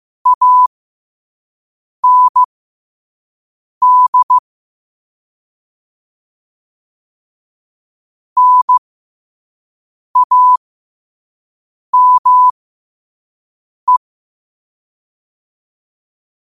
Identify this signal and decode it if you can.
Morse code, heard as a keyed tone, about 15 wpm, spaced as for 5 wpm, 1000 Hz. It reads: AND NAME
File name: qotdfiles / 5wpm